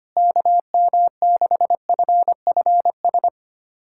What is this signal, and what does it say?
Morse code, heard as a keyed tone, about 25 wpm, 700 Hz. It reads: KM6FFS
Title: USA Callsign DE Callsign 25wpm